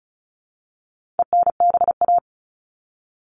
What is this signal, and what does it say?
Morse code, heard as a keyed tone, about 34 wpm, 700 Hz. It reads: ENBA
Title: All Cut Numbers - 4-Digit 35wpm